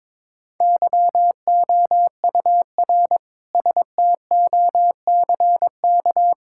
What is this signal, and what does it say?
Morse code, heard as a keyed tone, about 22 wpm, 700 Hz. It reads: YOUR STOCK
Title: Sentences from Top 1000 Words - Review Part 5 - 22wpm